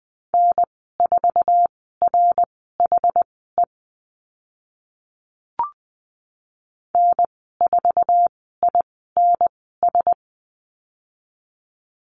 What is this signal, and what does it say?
Morse code, heard as a keyed tone, about 20 wpm, spaced as for 15 wpm, 700 Hz. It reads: N4RHE N4INS
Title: 082 Lesson 14g - Number 2 - Call Signs 20wpm